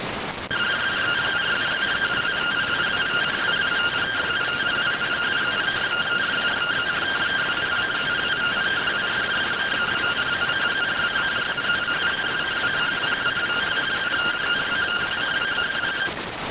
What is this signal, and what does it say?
Two-tone teleprinter signal, about 170 Hz shift.